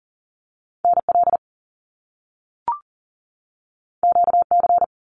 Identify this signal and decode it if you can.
Morse code, heard as a keyed tone, about 40 wpm, 700 Hz. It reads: NL QC
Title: Canadian Provinces and Territories 40wpm